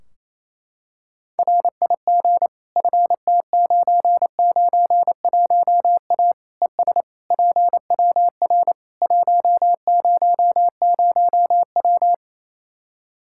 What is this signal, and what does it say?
Morse code, heard as a keyed tone, about 28 wpm, 700 Hz. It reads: RIG FT991A ES PWR 100W
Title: Practice QSO - Exchange 2 - Info - 28wpm